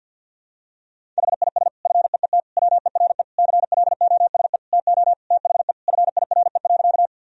Morse code, heard as a keyed tone, about 46 wpm, 700 Hz. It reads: HIS FEET WERE CLOSE TO THE FIRE.
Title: Sentences from Top 1000 Words - Review Part 1 - 50wpm